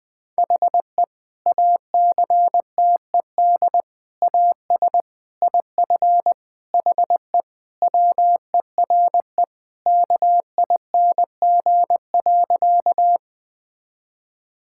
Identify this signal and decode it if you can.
Morse code, heard as a keyed tone, about 21 wpm, 700 Hz. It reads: HE ACTED AS IF HE WERE KING.
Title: Sentences with 4-Word Phrases - 20wpm